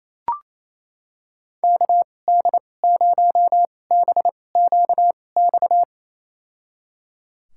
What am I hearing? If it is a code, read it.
Morse code, heard as a keyed tone, about 28 wpm, spaced as for 20 wpm, 700 Hz. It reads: KD0BQX
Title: USA Call Signs - 6 Characters - 2x Character Spacing - 28wpm